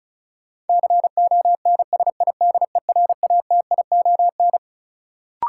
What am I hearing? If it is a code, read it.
Morse code, heard as a keyed tone, about 34 wpm, 700 Hz. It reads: CONSIDERATION
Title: Sentences from Top 1500 Words - New Word Drill - 35wpm